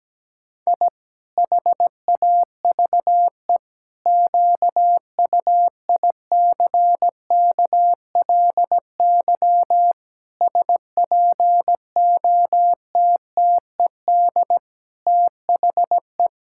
Morse code, heard as a keyed tone, about 17 wpm, 700 Hz. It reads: I HAVE QUICKLY SPOTTED THE
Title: Pangram Challenge v2 - 17wpm